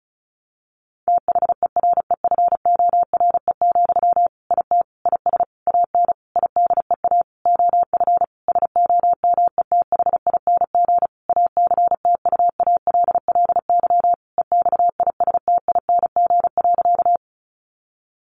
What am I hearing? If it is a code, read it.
Morse code, heard as a keyed tone, about 34 wpm, 700 Hz. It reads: THEREFORE, IT IS AN IDEA OF SOMETHING ACTUALLY EXISTING.
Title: Sentences from Top 1500 Words 35wpm